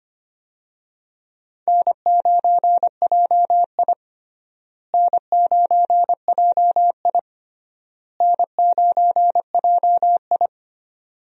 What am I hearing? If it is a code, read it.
Morse code, heard as a keyed tone, about 25 wpm, 700 Hz. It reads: N9JI N9JI N9JI
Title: USA Call Signs - 4 Characters - Repeated 3x - 25wpm